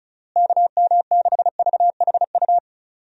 Morse code, heard as a keyed tone, about 34 wpm, 700 Hz. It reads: KM6VHU
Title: USA Call Signs - 6 Characters - Repeated 3x - 35wpm